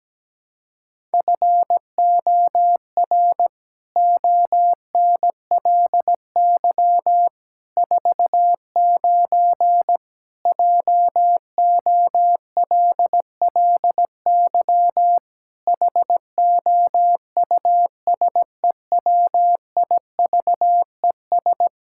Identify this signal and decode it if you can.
Morse code, heard as a keyed tone, about 17 wpm, 700 Hz. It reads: FOR ONLY 49 JOLLY HOUSEWIVES
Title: Pangram Challenge v2 - 17wpm